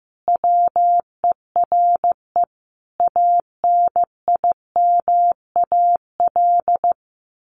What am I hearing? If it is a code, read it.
Morse code, heard as a keyed tone, about 14 wpm, 700 Hz. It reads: WERE ANIMAL
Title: Sentences from Top 1300 Words 15wpm